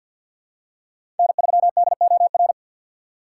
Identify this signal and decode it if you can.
Morse code, heard as a keyed tone, about 53 wpm, 700 Hz. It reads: N3DOR
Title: USA Call Signs - 5 Characters - 6x Character Spacing - 50wpm